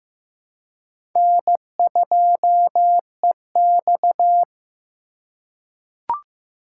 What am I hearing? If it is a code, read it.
Morse code, heard as a keyed tone, about 15 wpm, 700 Hz. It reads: N2EX
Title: USA Call Signs - 4x Character Spacing - 15wpm